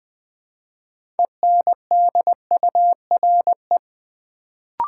Morse code, heard as a keyed tone, about 20 wpm, 700 Hz. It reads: ENDURE
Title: Build A Word - 3 Letters At A Time - 6-Letter Words - 20wpm